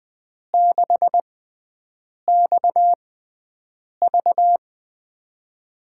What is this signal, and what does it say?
Morse code, heard as a keyed tone, about 21 wpm, spaced as for 7 wpm, 700 Hz. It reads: 6XV
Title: USA Call Signs - 6x Character Spacing - 20wpm